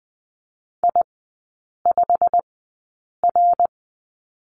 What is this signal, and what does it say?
Morse code, heard as a keyed tone, about 20 wpm, spaced as for 8 wpm, 700 Hz. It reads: I5R
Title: 178 Lesson 29e - Slash - Three Characters 20wpm